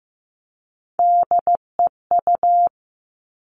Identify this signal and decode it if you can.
Morse code, heard as a keyed tone, about 15 wpm, 700 Hz. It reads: DEU
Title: Three Challenging Letter-Number Combinations 15wpm